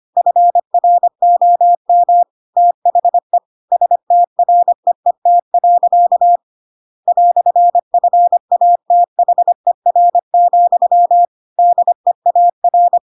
Morse code, heard as a keyed tone, about 25 wpm, 700 Hz. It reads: FROM THE STREET. "FATHER, DEAR
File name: Pinocchio / Pinocchio ch7 25 25 ik1hge